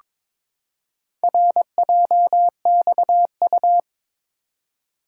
Morse code, heard as a keyed tone, about 22 wpm, 700 Hz. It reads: RJXU
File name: practice / 22wpm / Four Letter-Number Combinations - 22wpm